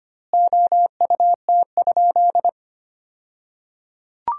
Morse code, heard as a keyed tone, about 25 wpm, 700 Hz. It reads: OUT?
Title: 2x Word-Spacing - Sentences from Top 300 Words - 25wpm